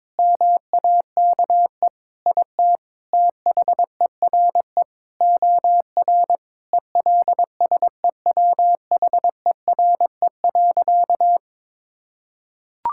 Morse code, heard as a keyed tone, about 22 wpm, 700 Hz. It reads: MAKE IT THERE OR ELSEWHERE.
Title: Sentences from Top 1600 Words 22wpm